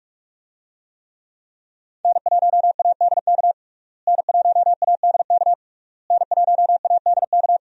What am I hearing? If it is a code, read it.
Morse code, heard as a keyed tone, about 42 wpm, 700 Hz. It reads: N1ADK N1ADK N1ADK
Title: USA Call Signs - 5 Characters - Repeated 3x - 45wpm